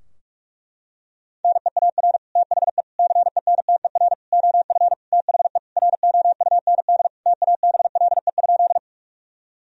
Morse code, heard as a keyed tone, about 46 wpm, 700 Hz. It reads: NEAR THE CENTER OF THE ROUND TABLE?
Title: Sentences from Top 1000 Words - Review Part 2 - 45wpm